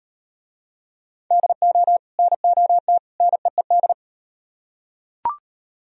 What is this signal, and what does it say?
Morse code, heard as a keyed tone, about 36 wpm, 700 Hz. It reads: DO NOT NEED